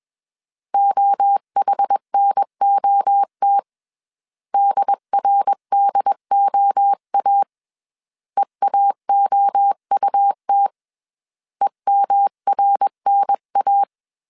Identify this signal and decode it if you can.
Morse code, heard as a keyed tone, about 21 wpm, 800 Hz. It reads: OHNOT DRDOA EAOUT EMRNA
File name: 20wpm